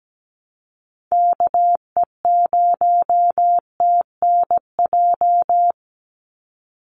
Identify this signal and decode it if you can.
Morse code, heard as a keyed tone, about 17 wpm, 700 Hz. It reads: KE0TNJ
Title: USA Call Signs 17wpm